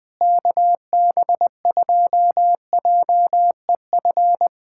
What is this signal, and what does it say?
Morse code, heard as a keyed tone, about 20 wpm, 700 Hz. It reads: KB2JEF
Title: USA Callsign DE Callsign - 6 Characters - 20wpm